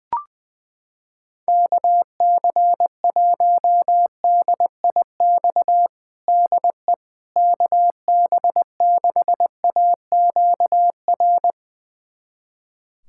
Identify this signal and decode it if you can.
Morse code, heard as a keyed tone, about 20 wpm, 700 Hz. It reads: KC1DIX DE KB6AQR
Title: USA Callsign DE Callsign 20wpm